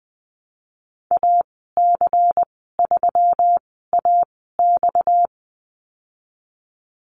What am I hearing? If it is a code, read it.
Morse code, heard as a keyed tone, about 20 wpm, spaced as for 15 wpm, 700 Hz. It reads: AC3AX
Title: 231 Lesson 37g - Letter x - Call Signs 20wpm